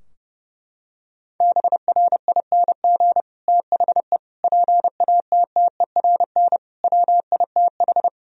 Morse code, heard as a keyed tone, about 30 wpm, 700 Hz. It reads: BRING THE PATTERN WITH
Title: Sentences from Top 1000 Words - Review Part 2 - 30wpm